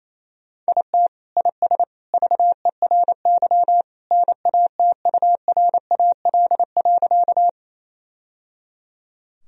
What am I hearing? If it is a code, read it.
Morse code, heard as a keyed tone, about 28 wpm, 700 Hz. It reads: IT IS VERY NATURAL.
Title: Sentences from Top 1000 Words - Review Part 4 - 28wpm